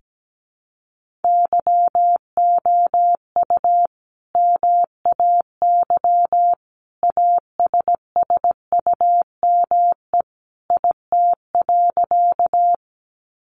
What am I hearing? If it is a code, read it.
Morse code, heard as a keyed tone, about 17 wpm, 700 Hz. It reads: YOU MAY ASSUME IT.
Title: Sentences from Top 1500 Words 17wpm